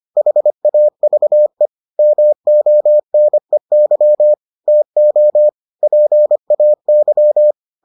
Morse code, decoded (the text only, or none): HAVE MONEY TO PAY